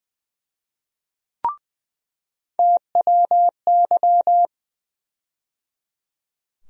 Morse code, heard as a keyed tone, about 20 wpm, 700 Hz. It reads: TWY
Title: USA Call Sign Postfixes - 3 Letters 20wpm